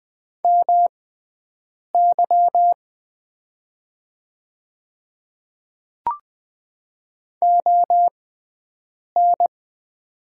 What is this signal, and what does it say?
Morse code, heard as a keyed tone, about 20 wpm, spaced as for 7 wpm, 700 Hz. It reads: MY ON
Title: 6x Character Spacing - Top 200 Words - 20wpm